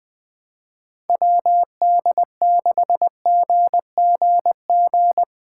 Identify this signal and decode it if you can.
Morse code, heard as a keyed tone, about 20 wpm, 700 Hz. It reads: WD6GGG
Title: USA Call Signs - 6 Characters - 2x Character Spacing - 20wpm